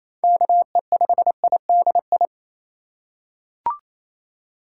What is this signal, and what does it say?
Morse code, heard as a keyed tone, about 27 wpm, 700 Hz. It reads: KE5IDI
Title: USA Call Signs - 6 Characters - 4x Character Spacing - 28wpm